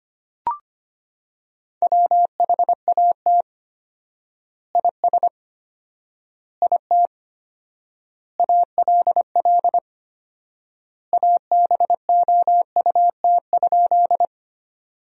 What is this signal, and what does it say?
Morse code, heard as a keyed tone, about 25 wpm, 700 Hz. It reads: WHAT IS IT ALL ABOUT?
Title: 4x Word-Spacing - Sentences from Top 100 Words - 25wpm